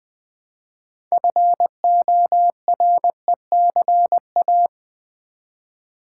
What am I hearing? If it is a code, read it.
Morse code, heard as a keyed tone, about 20 wpm, 700 Hz. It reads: FORECA
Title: Build A Word - 3 Letters At A Time - 8-Letter Words - 20wpm